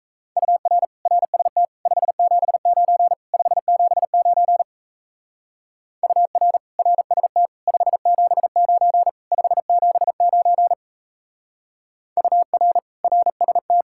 Morse code, heard as a keyed tone, about 34 wpm, 700 Hz. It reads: UR RST 579 579 UR RST 579 579 UR RST
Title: Speed-Racing Common QSO Phrases - 28wpm